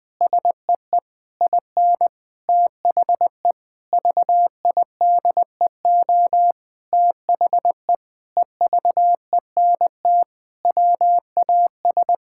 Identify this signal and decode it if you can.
Morse code, heard as a keyed tone, about 20 wpm, 700 Hz. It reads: SEE IN THE VIDEO THE EVENT WAS
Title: Sentences with 7-Word Phrases - Spoken Beginning - 20wpm